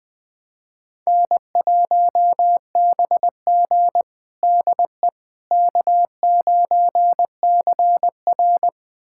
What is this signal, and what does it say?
Morse code, heard as a keyed tone, about 20 wpm, 700 Hz. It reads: N1BG DE K9CR
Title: USA Callsign DE Callsign - 4 Characters - 20wpm